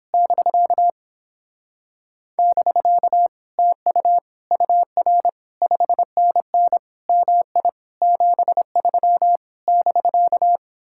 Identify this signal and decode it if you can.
Morse code, heard as a keyed tone, about 29 wpm, 700 Hz. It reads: <BK> <BK> TU UR 5NN MI 73 <BK>